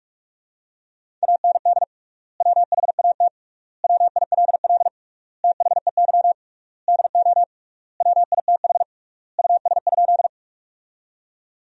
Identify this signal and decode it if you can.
Morse code, heard as a keyed tone, about 46 wpm, 700 Hz. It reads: AND WHAT WILL THEY DO WITH US?
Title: 3x Word-Spacing - Sentences from Top 200 Words - 45wpm